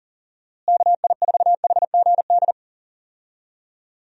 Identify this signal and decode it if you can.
Morse code, heard as a keyed tone, about 40 wpm, 700 Hz. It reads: KI4HGD
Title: USA Call Signs - 6 Characters - Repeated 2x - 40wpm